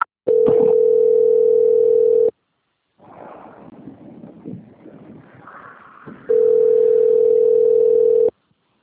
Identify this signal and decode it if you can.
Ringback tone (North America), 440 + 480 Hz.
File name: out- - - - - .13005